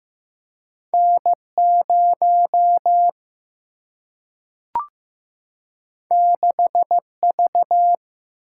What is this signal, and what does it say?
Morse code, heard as a keyed tone, about 15 wpm, 700 Hz. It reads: N0 6V
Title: Single-Letter Single-Number 15wpm